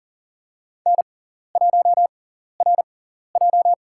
Morse code, heard as a keyed tone, about 40 wpm, spaced as for 14 wpm, 700 Hz. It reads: N1RJ
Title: USA Call Signs - 4 Characters - 6x Character Spacing - 40wpm